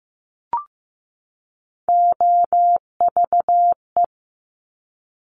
Morse code, heard as a keyed tone, about 15 wpm, 700 Hz. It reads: OVE
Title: Top Three Letter Combinations in Words 15wpm